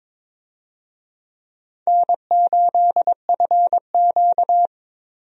Morse code, heard as a keyed tone, about 22 wpm, 700 Hz. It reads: N8FQ
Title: USA Call Signs - 4 Characters - Repeated 3x - 22wpm